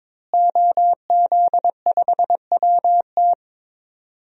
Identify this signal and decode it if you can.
Morse code, heard as a keyed tone, about 22 wpm, 700 Hz. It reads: OZ5WT
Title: DXCC-100 Call Signs - 5 Characters - Repeated 3x - 22wpm